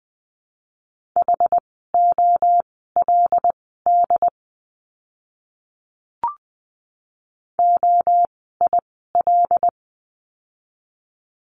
Morse code, heard as a keyed tone, about 20 wpm, spaced as for 14 wpm, 700 Hz. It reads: HOLD OIL
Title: 075 Lesson 13f - Letter L - Words 20wpm